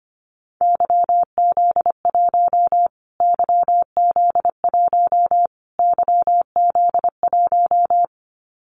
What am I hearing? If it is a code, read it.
Morse code, heard as a keyed tone, about 25 wpm, 700 Hz. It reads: YZ1 YZ1 YZ1
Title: Callsign Prefixes for Top 25 DXCC - Repeated 3x - 25wpm